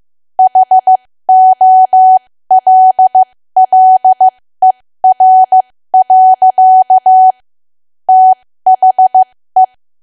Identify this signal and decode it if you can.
Morse code, heard as a keyed tone, about 15 wpm, spaced as for 13 wpm, 750 Hz. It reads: HOLLER. THE
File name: cwa / W1AW-Level2 / 13WPM